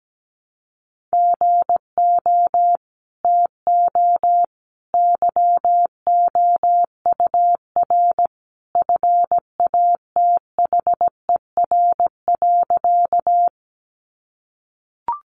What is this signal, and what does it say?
Morse code, heard as a keyed tone, about 17 wpm, 700 Hz. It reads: GO TO YOUR FATHER.
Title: Sentences from Top 300 Words 17wpm